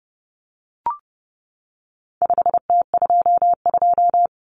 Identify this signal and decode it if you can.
Morse code, heard as a keyed tone, about 30 wpm, 700 Hz. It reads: 5T22
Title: Common Cut Numbers - 4-Digit 30wpm